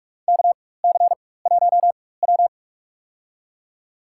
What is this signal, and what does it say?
Morse code, heard as a keyed tone, about 44 wpm, spaced as for 21 wpm, 700 Hz. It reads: KC1W
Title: USA Call Signs - 4 Characters - 4x Character Spacing - 45wpm